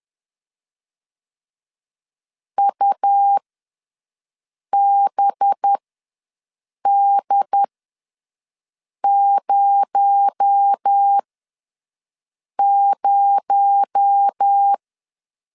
Morse code, heard as a keyed tone, about 10 wpm, spaced as for 5 wpm, 800 Hz. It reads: UBD00
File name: LEZIONI / 6 PKX0 / 6 PKX0 ws 10wpm 4220